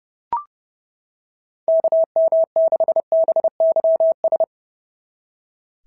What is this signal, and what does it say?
Morse code, heard as a keyed tone, about 30 wpm, 690 Hz. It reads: KM6BYS
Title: SOTA QSO Practice 30wpm